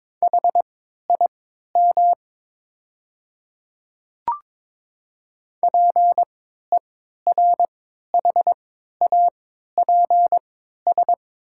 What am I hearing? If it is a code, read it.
Morse code, heard as a keyed tone, about 22 wpm, spaced as for 12 wpm, 700 Hz. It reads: HIM PERHAPS
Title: 3x Character Spacing - Top 500 Words - 22wpm